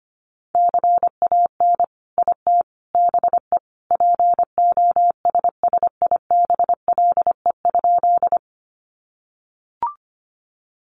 Morse code, heard as a keyed tone, about 25 wpm, 700 Hz. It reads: CAN IT BE POSSIBLE?
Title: Sentences from Top 500 Words 25wpm